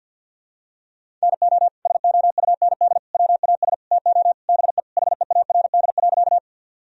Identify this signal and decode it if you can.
Morse code, heard as a keyed tone, about 47 wpm, 700 Hz. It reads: NO SOUND WAS TO BE HEARD.
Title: Sentences from Top 1000 Words - Review Part 1 - 50wpm